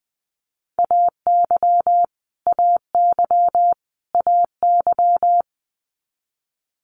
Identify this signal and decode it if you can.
Morse code, heard as a keyed tone, about 20 wpm, 700 Hz. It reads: AY AY AY
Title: Multi-letter Phonemes Repeated 3x 20wpm